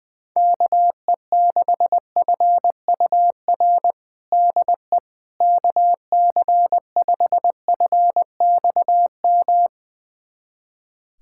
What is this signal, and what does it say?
Morse code, heard as a keyed tone, about 21 wpm, 700 Hz. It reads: KE6FUR DE KC5FXM
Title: USA Callsign DE Callsign 20wpm